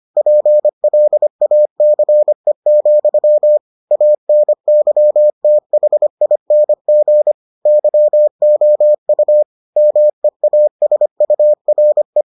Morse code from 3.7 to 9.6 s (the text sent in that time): ANYTHING YOU